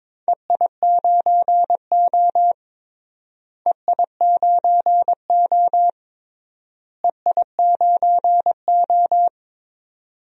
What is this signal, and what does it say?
Morse code, heard as a keyed tone, about 22 wpm, 700 Hz. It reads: EI9O EI9O EI9O
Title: DXCC-100 Call Signs - 4 Characters - Repeated 3x - 22wpm